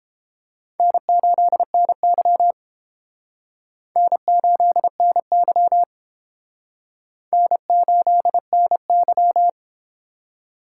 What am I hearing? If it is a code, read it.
Morse code, heard as a keyed tone, about 29 wpm, 700 Hz. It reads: N8NY N8NY N8NY